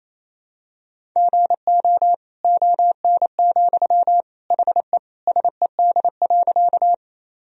Morse code, heard as a keyed tone, about 28 wpm, 700 Hz. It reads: GO ON, HE SED.
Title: Sentences from Top 100 Words and Abbreviations 28wpm